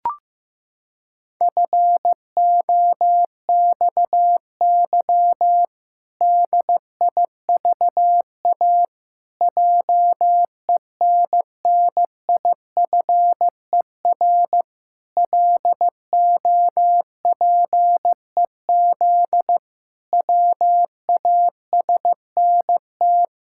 Morse code, heard as a keyed tone, about 15 wpm, 700 Hz. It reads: FOXY DIVA JENNIFER LOPEZ WASNT
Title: Pangram Challenge v2 - 15wpm